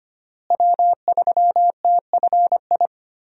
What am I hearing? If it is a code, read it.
Morse code, heard as a keyed tone, about 25 wpm, 700 Hz. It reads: W3TFI
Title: USA Call Signs - 5 Characters - Repeated 3x - 25wpm